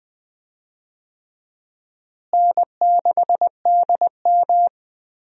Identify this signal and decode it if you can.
Morse code, heard as a keyed tone, about 20 wpm, 700 Hz. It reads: N6DM
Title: USA Call Signs - 4 Characters - Repeated 3x - 20wpm